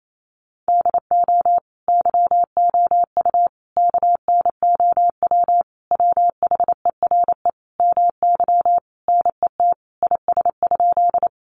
Morse code, heard as a keyed tone, about 28 wpm, 700 Hz. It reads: DO YOU KNOW WHERE MY NET IS?
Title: Sentences from Top 1400 Words 28wpm